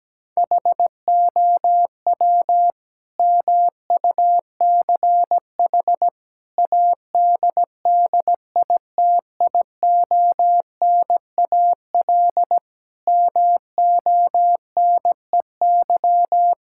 Morse code, heard as a keyed tone, about 17 wpm, 700 Hz. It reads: HOW MUCH ADDITIONAL MONEY
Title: Sentences from Top 1000 Words - Review Part 5 - 17wpm